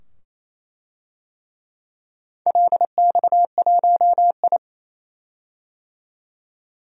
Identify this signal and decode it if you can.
Morse code, heard as a keyed tone, about 28 wpm, 700 Hz. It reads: LX1I
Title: DXCC-100 Call Signs - 4 Characters - 28wpm